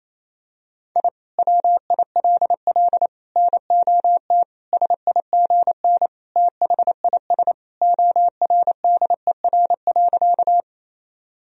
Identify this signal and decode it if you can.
Morse code, heard as a keyed tone, about 28 wpm, 700 Hz. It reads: I WILL NOT SIGN THIS ORDER.
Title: Sentences from Top 1000 Words - Review Part 2 - 28wpm